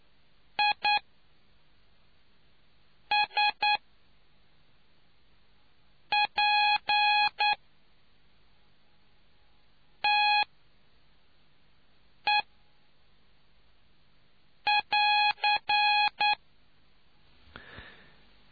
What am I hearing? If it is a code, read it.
Morse code, heard as a keyed tone, about 9 wpm, spaced as for 3 wpm, 830 Hz. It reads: ISPTE <AR>